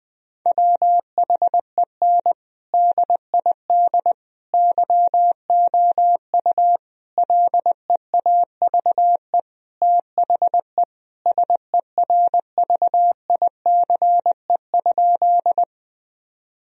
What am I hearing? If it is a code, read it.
Morse code, heard as a keyed tone, about 20 wpm, 700 Hz. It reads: WHEN DID YOU LEAVE THE SERVICE?
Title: Sentences from Top 1000 Words - Review Part 4 - 20wpm